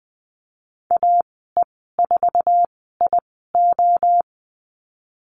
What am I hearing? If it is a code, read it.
Morse code, heard as a keyed tone, about 19 wpm, spaced as for 14 wpm, 700 Hz. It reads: AE4IO
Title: 048 Lesson 9g - Number 4 - Call Signs 20wpm